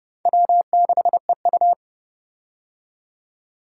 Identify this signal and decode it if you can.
Morse code, heard as a keyed tone, about 30 wpm, 700 Hz. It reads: W6EU
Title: K1USN SST QSO Practice - 30wpm